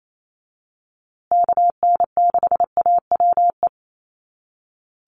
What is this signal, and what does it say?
Morse code, heard as a keyed tone, about 28 wpm, 700 Hz. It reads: KN6AWE
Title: USA Call Signs Repeated 3x 28wpm